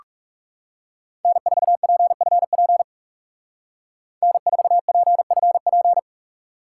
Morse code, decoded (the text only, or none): N4PFP N4PFP